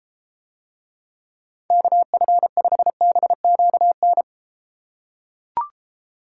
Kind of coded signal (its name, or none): Morse code